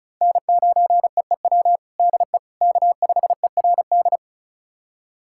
Morse code, heard as a keyed tone, about 36 wpm, 700 Hz. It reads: N9EEW DE K5ERD
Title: USA Callsign DE Callsign - 5 Characters - 35wpm